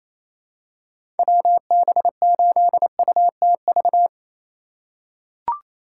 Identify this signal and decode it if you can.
Morse code, heard as a keyed tone, about 28 wpm, 700 Hz. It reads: WB8UTV
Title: USA Call Signs - 6 Characters - 4x Character Spacing - 28wpm